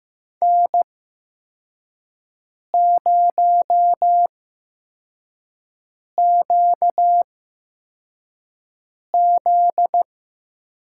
Morse code, heard as a keyed tone, about 15 wpm, spaced as for 4 wpm, 700 Hz. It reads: N0QZ
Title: USA Call Signs - 4 Characters - 8x Character Spacing - 15wpm